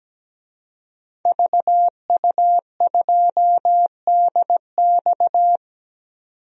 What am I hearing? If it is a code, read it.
Morse code, heard as a keyed tone, about 17 wpm, 700 Hz. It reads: VU2DX
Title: DXCC-100 Call Signs - 5 Characters - 17wpm